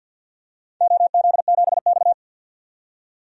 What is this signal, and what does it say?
Morse code, heard as a keyed tone, about 51 wpm, 700 Hz. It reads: OZ7X
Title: DXCC-100 Call Signs - 4 Characters - 50wpm